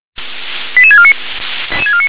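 Five-tone selective call (ZVEI), heard as quick runs of five tones, tones of about 70 ms. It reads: 88649
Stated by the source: ZVEI five-tone selective call received on an SDR.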